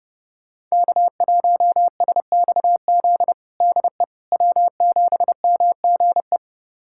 Morse code, heard as a keyed tone, about 30 wpm, 700 Hz. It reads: K1SXZ DE W7MGE
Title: USA Callsign DE Callsign - 5 Characters - 30wpm